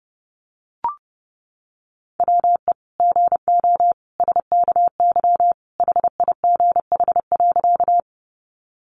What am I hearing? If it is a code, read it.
Morse code, heard as a keyed tone, about 30 wpm, 700 Hz. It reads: WE GO SKY HIGH.
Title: Sentences from Top 1200 Words 30wpm